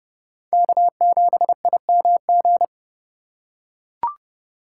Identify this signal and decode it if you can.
Morse code, heard as a keyed tone, about 29 wpm, 700 Hz. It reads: K7IMG